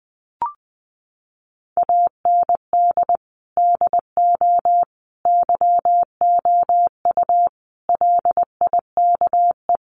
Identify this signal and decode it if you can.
Morse code, heard as a keyed tone, about 19 wpm, 700 Hz. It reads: AND DO YOU LIKE
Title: Sentences from Top 100 Words 20wpm